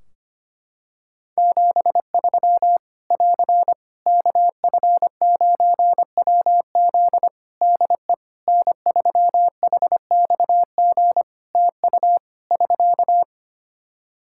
Morse code, heard as a keyed tone, about 25 wpm, 700 Hz. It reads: 73 <AR> KF9WZ DE N3HXG TU <SK>
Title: Practice QSO - Exchange 4 - Ending - 25wpm